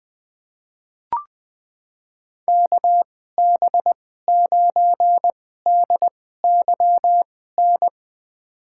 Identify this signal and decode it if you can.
Morse code, heard as a keyed tone, about 20 wpm, spaced as for 14 wpm, 700 Hz. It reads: KB9DYN
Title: USA Call Signs - 6 Characters - 2x Character Spacing - 20wpm